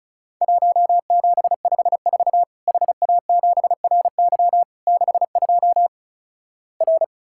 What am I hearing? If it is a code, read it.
Morse code, heard as a keyed tone, about 36 wpm, 700 Hz. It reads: 1754 HA7RY 62 R
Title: Worked All Europe DX Contest - QTC Exchange - 35wpm